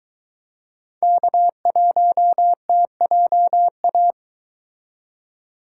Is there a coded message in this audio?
Yes: Morse code from 1.0 to 4.1 s, heard as a keyed tone, about 22 wpm, 700 Hz. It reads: K1TJA